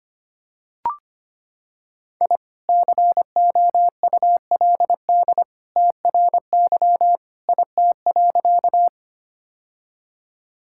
Morse code, heard as a keyed tone, about 25 wpm, 690 Hz. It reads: I COULD TRY IT.
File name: practice / 25wpm / Sentences with 2-Word Phrases - 25wpm